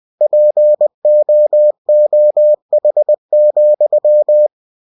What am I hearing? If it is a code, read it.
Morse code, heard as a keyed tone, about 20 wpm, 600 Hz. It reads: POOH,
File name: cw-books / winnie-the-pooh / winnie-20wpm-page-0099